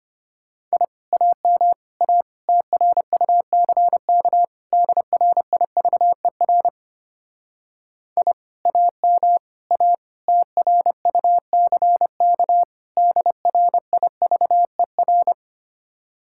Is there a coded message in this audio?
Yes: Morse code from 0.7 to 15.3 s, heard as a keyed tone, about 33 wpm, 700 Hz. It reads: I AM A TRUCK DRIVER I AM A TRUCK DRIVER